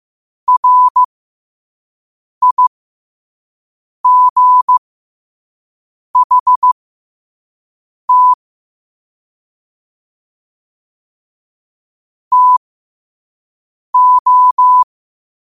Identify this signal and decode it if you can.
Morse code, heard as a keyed tone, about 15 wpm, spaced as for 5 wpm, 1000 Hz. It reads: RIGHT TO